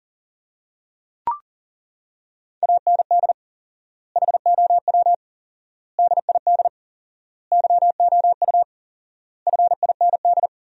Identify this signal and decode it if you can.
Morse code, heard as a keyed tone, about 40 wpm, 700 Hz. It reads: AND HOW DID YOU FIND
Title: 4x Word-Spacing - Sentences from Top 100 Words - 40wpm